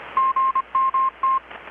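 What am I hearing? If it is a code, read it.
Morse code, heard as a keyed tone, about 25 wpm, 1060 Hz. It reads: GMT
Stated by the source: Morse from KPH's Night of Nights XX maritime radiotelegraphy event, received in Chiapas, Mexico about 3,800 km away on a Kenwood R-600 with a grounded 10 m long-wire antenna.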